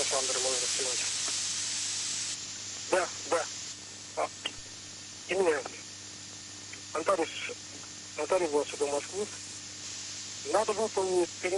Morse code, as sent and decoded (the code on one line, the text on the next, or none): none